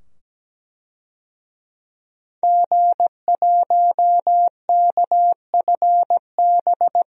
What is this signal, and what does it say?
Morse code, heard as a keyed tone, about 17 wpm, 700 Hz. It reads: G1KFB
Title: DXCC-100 Call Signs - 5 Characters - 17wpm